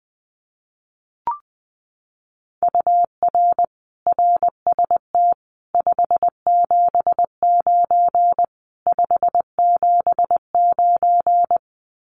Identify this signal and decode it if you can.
Morse code, heard as a keyed tone, about 20 wpm, 700 Hz. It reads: UR RST 579 579
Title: Common QSO Phrases Encore 20wpm